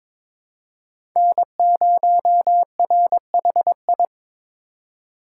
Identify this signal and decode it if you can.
Morse code, heard as a keyed tone, about 22 wpm, 700 Hz. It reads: N0RHI
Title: USA Call Signs - 5 Characters - 2x Character Spacing - 22wpm